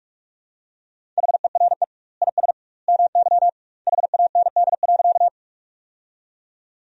Morse code, heard as a keyed tone, about 43 wpm, 700 Hz. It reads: HERE IS MY HAND.
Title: 2x Word-Spacing - Sentences from Top 200 Words - 45wpm